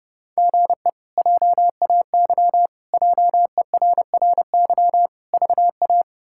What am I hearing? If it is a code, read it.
Morse code, heard as a keyed tone, about 32 wpm, 700 Hz. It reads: GE JAY JERRY VA
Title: K1USN SST QSO Practice - 30wpm